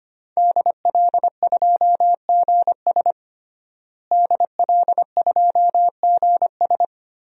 Morse code, heard as a keyed tone, about 25 wpm, 700 Hz. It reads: DL2GS DL2GS
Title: DXCC-100 Call Signs - 5 Characters - Repeated 3x - 25wpm